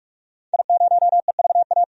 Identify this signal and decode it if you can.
Morse code, heard as a keyed tone, about 44 wpm, 700 Hz. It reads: I0EVA
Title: DXCC-100 Call Signs - 5 Characters - 45wpm